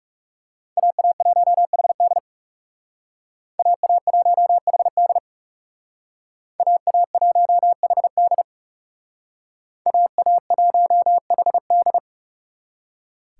Morse code, heard as a keyed tone, about 38 wpm, 700 Hz. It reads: AA1HD AA1HD AA1HD AA1HD